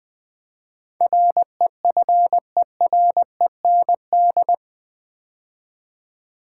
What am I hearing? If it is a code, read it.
Morse code, heard as a keyed tone, about 20 wpm, 700 Hz. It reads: REFEREND
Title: Build A Word - 4 Letters At A Time - 10-Letter Words - 20wpm